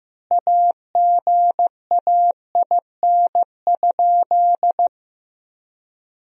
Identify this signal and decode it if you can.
Morse code, heard as a keyed tone, about 15 wpm, 700 Hz. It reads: AGAIN?
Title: 4x Word-Spacing - Sentences from Top 200 Words - 15wpm